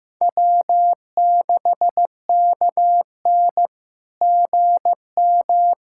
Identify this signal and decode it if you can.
Morse code, heard as a keyed tone, about 15 wpm, 690 Hz. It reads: W6KN GM
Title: SOTA QSO Practice 15wpm